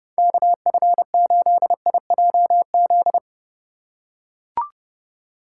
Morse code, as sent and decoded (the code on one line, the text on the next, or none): -.- ..-. ---.. .. .--- --..
KF8IJZ